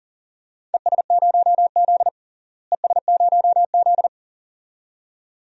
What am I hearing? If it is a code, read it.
Morse code, heard as a keyed tone, about 40 wpm, 700 Hz. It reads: ES0Z ES0Z
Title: DXCC-100 Call Signs - 4 Characters - Repeated 2x - 40wpm